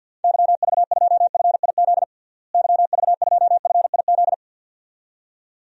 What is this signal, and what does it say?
Morse code, heard as a keyed tone, about 49 wpm, 700 Hz. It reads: YV2FIZ YV2FIZ
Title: DXCC-100 Call Signs - 6 Characters - Repeated 2x - 50wpm